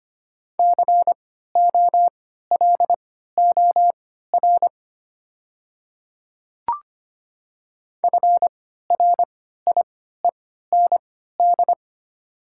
Morse code, heard as a keyed tone, about 25 wpm, spaced as for 14 wpm, 700 Hz. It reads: COLOR FRIEND
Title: 3x Character Spacing - Top 300 Words - 25wpm